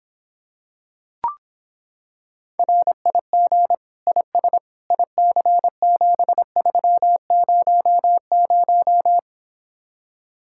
Morse code, heard as a keyed tone, about 26 wpm, 700 Hz. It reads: RIG IS IC7300